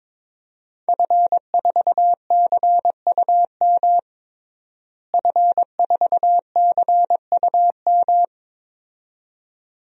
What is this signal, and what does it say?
Morse code, heard as a keyed tone, about 22 wpm, 700 Hz. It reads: F4CUM F4CUM
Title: DXCC-100 Call Signs - 5 Characters - Repeated 2x - 22wpm